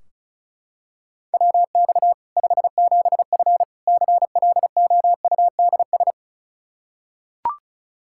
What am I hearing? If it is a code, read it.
Morse code, heard as a keyed tone, about 36 wpm, 700 Hz. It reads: WX 57F CLOUDS
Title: K1USN SST QSO Practice 35wpm